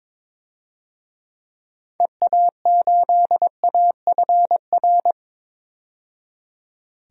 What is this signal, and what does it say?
Morse code, heard as a keyed tone, about 22 wpm, 700 Hz. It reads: EA8AFR
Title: DXCC-100 Call Signs - 6 Characters - 22wpm